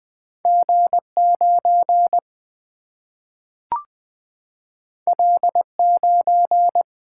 Morse code, heard as a keyed tone, about 20 wpm, 700 Hz. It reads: G9 L9
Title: Single-Letter Single-Number 20wpm